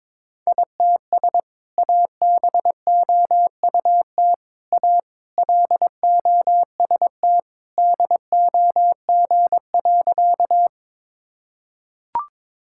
Morse code, heard as a keyed tone, about 22 wpm, 700 Hz. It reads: ITS ABOUT A LOST DOG.
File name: practice / 22wpm / Sentences with 2-Word Phrases - 22wpm